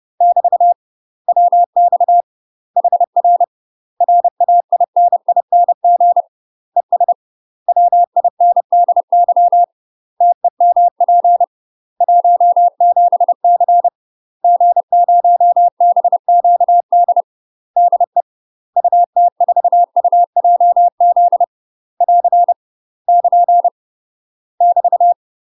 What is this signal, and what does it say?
Morse code, heard as a keyed tone, about 30 wpm, 700 Hz. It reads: <BT> WX HR RAINING ES WINDY TEMP 17C G0BQD DE UT4UJZ <AR> <KN> <BT>
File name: UK MASTER QSOs TRAINING / 30WPM / UK MASTER QSOs 0026